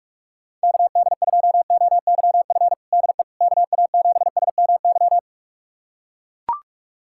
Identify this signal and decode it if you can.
Morse code, heard as a keyed tone, about 41 wpm, 700 Hz. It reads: KD2OYF DE KA7SMY
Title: USA Callsign DE Callsign - 6 Characters - 45wpm